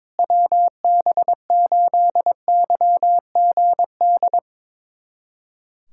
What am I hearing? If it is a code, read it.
Morse code, heard as a keyed tone, about 22 wpm, 700 Hz. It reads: WB8YGD
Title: USA Callsign DE Callsign 22wpm